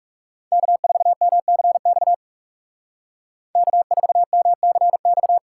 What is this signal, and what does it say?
Morse code, heard as a keyed tone, about 35 wpm, 700 Hz. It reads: K4MCX K4MCX
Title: Speed Racing - USA 1x3 Call Signs 30wpm